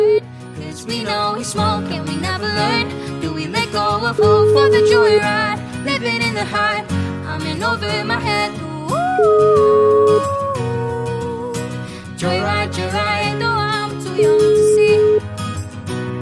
Ringback tone (Europe, 425 Hz), with noise.